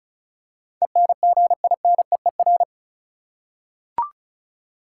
Morse code, heard as a keyed tone, about 36 wpm, 700 Hz. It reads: ENGINEER
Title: Practice QSO - Exchange 3 - More Info - 35wpm